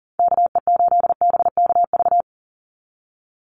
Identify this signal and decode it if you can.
Morse code, heard as a keyed tone, about 38 wpm, 700 Hz. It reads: KE8BKV
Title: USA Call Signs 40wpm